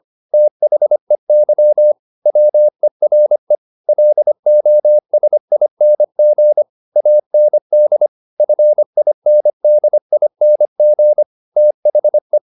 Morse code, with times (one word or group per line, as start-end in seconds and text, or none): 0.3-1.9 THEY
2.3-3.5 WERE
3.9-6.6 LOSING
7.0-8.1 AND
8.4-11.2 FINDING
11.6-12.4 THE